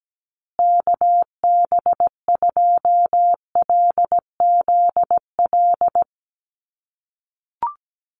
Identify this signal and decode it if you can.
Morse code, heard as a keyed tone, about 17 wpm, 700 Hz. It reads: KB2LZL
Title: USA Call Signs Repeated 3x 17wpm